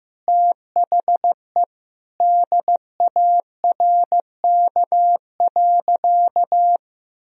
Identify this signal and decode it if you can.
Morse code, heard as a keyed tone, about 15 wpm, 700 Hz. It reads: THE DARK.
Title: Sentences from Top 1000 Words - Review Part 2 - 15wpm